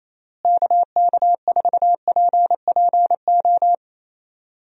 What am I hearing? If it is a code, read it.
Morse code, heard as a keyed tone, about 28 wpm, 700 Hz. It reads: KK4PPO
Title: USA Call Signs - 6 Characters - Repeated 2x - 28wpm